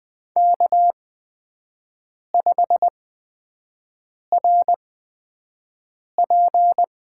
Morse code, heard as a keyed tone, about 20 wpm, spaced as for 5 wpm, 700 Hz. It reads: K5RP
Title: USA Call Signs - 8x Character Spacing - 20wpm